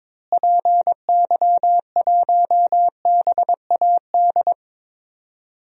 Morse code, heard as a keyed tone, about 22 wpm, 700 Hz. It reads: PY1BAD
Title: DXCC-100 Call Signs - 6 Characters - Repeated 3x - 22wpm